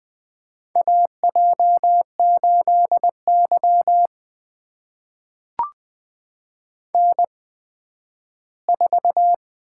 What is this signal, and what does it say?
Morse code, heard as a keyed tone, about 20 wpm, spaced as for 12 wpm, 700 Hz. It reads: AJ8Y N4
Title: USA Call Signs - 4 Characters - 8x Character Spacing - 20wpm